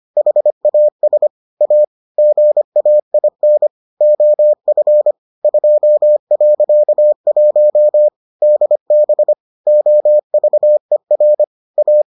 Morse code, heard as a keyed tone, about 25 wpm, 600 Hz. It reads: HAS A GAIN OF 2.1 DB OVER A